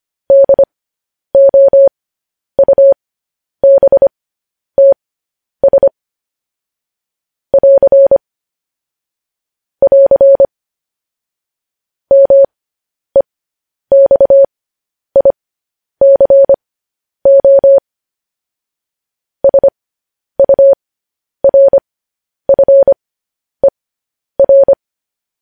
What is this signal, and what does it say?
Morse code, heard as a keyed tone, about 25 wpm, spaced as for 10 wpm, 550 Hz. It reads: DOUBTS <AR> <AR> MEXICO SURFER
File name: podcast / News headlines-1x-25x10